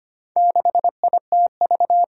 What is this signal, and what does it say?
Morse code, heard as a keyed tone, about 25 wpm, 700 Hz. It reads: 6ITV
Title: Four Challenging Letter-Number Combinations 25wpm